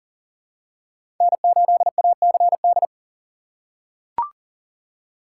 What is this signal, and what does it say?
Morse code, heard as a keyed tone, about 40 wpm, 700 Hz. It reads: N8ACD
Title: USA Call Signs - 5 Characters - 6x Character Spacing - 40wpm